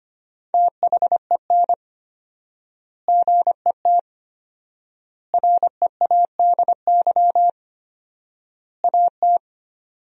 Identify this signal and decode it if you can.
Morse code, heard as a keyed tone, about 25 wpm, 700 Hz. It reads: THEN GET READY AT
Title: 4x Word-Spacing - Sentences from Top 400 Words - 25wpm